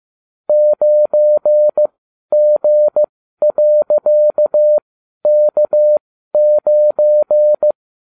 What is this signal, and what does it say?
Morse code, heard as a keyed tone, about 15 wpm, spaced as for 12 wpm, 600 Hz. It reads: 9G.K9